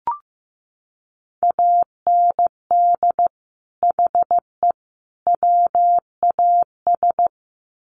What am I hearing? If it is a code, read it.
Morse code, heard as a keyed tone, about 15 wpm, 700 Hz. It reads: AND HE WAS
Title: Sets of 3 Words 15wpm